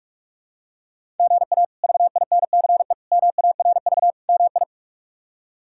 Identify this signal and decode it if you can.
Morse code, heard as a keyed tone, about 42 wpm, 700 Hz. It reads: GA VINCE MARV MI
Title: K1USN SST QSO Practice - 45wpm